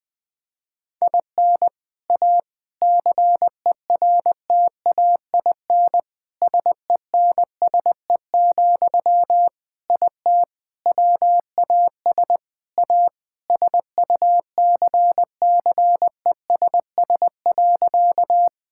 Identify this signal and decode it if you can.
Morse code, heard as a keyed tone, about 21 wpm, 700 Hz. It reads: IN A CERTAIN SENSE, IT WAS A SUCCESS.
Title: Sentences from Top 1000 Words - Review Part 6 - 20wpm